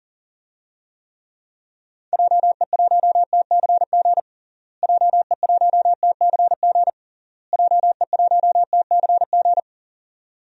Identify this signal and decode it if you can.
Morse code, heard as a keyed tone, about 40 wpm, 700 Hz. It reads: JE1TCG JE1TCG JE1TCG
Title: DXCC-100 Call Signs - 6 Characters - Repeated 3x - 40wpm